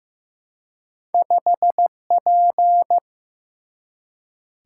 Morse code, heard as a keyed tone, about 15 wpm, 700 Hz. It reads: 5P
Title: Single-Letter Single-Number 15wpm